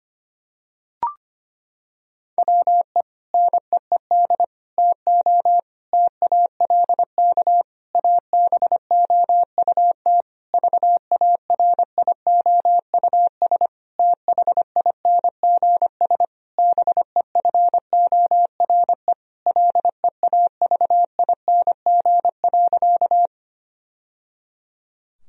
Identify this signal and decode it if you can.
Morse code, heard as a keyed tone, about 25 wpm, 700 Hz. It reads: WE NEED TO TALK ABOUT VARIOUS THINGS BEFORE LEAVING.
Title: Sentences from Top 1000 Words - Review Part 4 - 25wpm